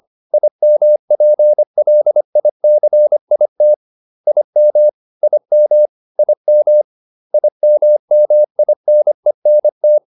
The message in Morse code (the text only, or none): IMPLICIT IM IM IM IMMINENT